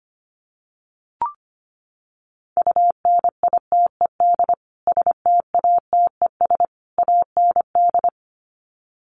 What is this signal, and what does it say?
Morse code, heard as a keyed tone, about 25 wpm, 700 Hz. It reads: UNITED STATES AND
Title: Sets of 3 Words 25wpm